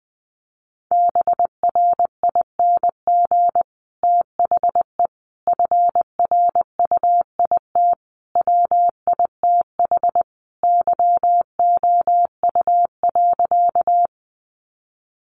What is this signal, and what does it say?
Morse code, heard as a keyed tone, about 20 wpm, 700 Hz. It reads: BRING THE FRUIT WITH YOU.
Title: Sentences from Top 1300 Words 20wpm